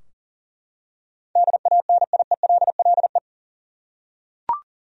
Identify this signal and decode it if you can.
Morse code, heard as a keyed tone, about 40 wpm, 700 Hz. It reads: DANIELLE
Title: USA Common Names - 40wpm